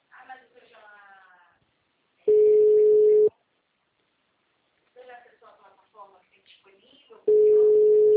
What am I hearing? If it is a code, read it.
Ringback tone (Europe, 425 Hz), with speech.
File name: out- -2008- - - .23102